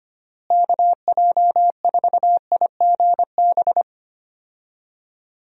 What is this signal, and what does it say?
Morse code, heard as a keyed tone, about 25 wpm, 700 Hz. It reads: KJ4IGB
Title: USA Call Signs - 6 Characters - Repeated 3x - 25wpm